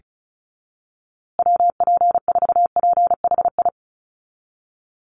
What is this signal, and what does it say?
Morse code, heard as a keyed tone, about 34 wpm, 700 Hz. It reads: WP4PHI
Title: USA Call Signs Repeated 3x 35wpm